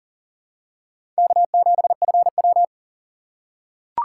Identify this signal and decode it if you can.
Morse code, heard as a keyed tone, about 38 wpm, 700 Hz. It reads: K7FW
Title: USA Call Signs - 4 Characters - 6x Character Spacing - 40wpm